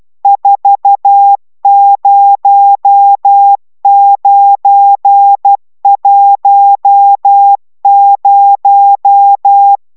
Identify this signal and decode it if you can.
Morse code, heard as a keyed tone, about 12 wpm, 800 Hz. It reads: 40910
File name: taviro tanulas / cw 75 31 10